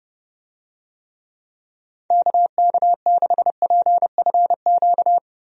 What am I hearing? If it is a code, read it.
Morse code, heard as a keyed tone, about 29 wpm, 700 Hz. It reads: KK6PFQ
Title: USA Call Signs - 6 Characters - Repeated 3x - 30wpm